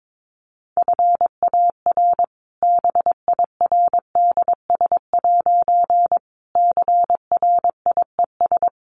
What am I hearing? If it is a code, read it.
Morse code, heard as a keyed tone, about 22 wpm, 700 Hz. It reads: FAR BIRDS' CRIES
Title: A Sound Of Thunder 22wpm